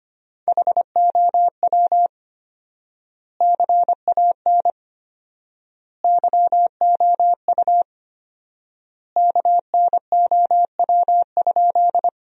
Morse code, heard as a keyed tone, about 25 wpm, 700 Hz. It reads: HOW CAN YOU KNOW?
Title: 4x Word-Spacing - Sentences from Top 100 Words - 25wpm